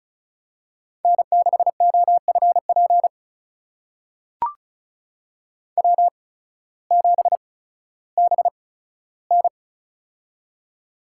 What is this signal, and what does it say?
Morse code, heard as a keyed tone, about 36 wpm, spaced as for 16 wpm, 700 Hz. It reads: N6OFP W7BN
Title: USA Call Signs - 8x Character Spacing - 35wpm